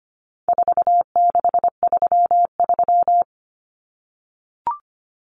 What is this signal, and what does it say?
Morse code, heard as a keyed tone, about 25 wpm, 700 Hz. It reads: 4633
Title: Common Cut Numbers - 4-Digit 25wpm